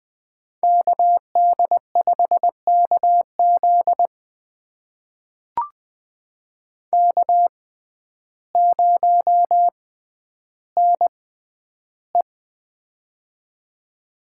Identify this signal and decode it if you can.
Morse code, heard as a keyed tone, about 20 wpm, spaced as for 11 wpm, 700 Hz. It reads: KD5KZ K0NE
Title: USA Call Signs - 6x Character Spacing - 20wpm